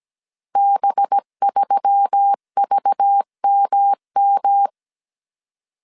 Morse code, heard as a keyed tone, about 17 wpm, 800 Hz. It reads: B3VMM
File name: 16wpm